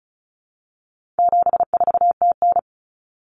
Morse code, heard as a keyed tone, about 34 wpm, 700 Hz. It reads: 74TN
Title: Common Cut Numbers - 4-Digit 35wpm